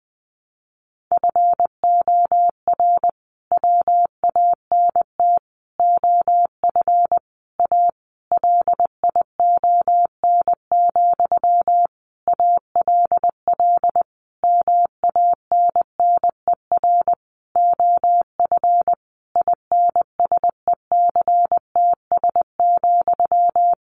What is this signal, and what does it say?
Morse code, heard as a keyed tone, about 20 wpm, 700 Hz. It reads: FOR WANT OF A LION, ALL MANNER OF INSECTS,
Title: A Sound Of Thunder 20wpm